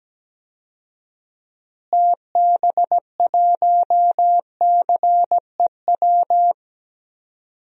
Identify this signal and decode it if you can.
Morse code, heard as a keyed tone, about 17 wpm, 700 Hz. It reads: TB1CEW
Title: DXCC-100 Call Signs - 6 Characters - Repeated 2x - 17wpm